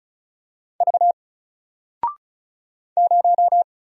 Morse code, heard as a keyed tone, about 34 wpm, 700 Hz. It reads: V 0
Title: Single Letter-Number 35wpm